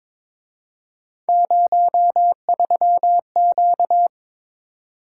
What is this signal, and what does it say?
Morse code, heard as a keyed tone, about 26 wpm, 700 Hz. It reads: 03Q
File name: Speed-Racing - Three Letter-Number Combinations - 22wpm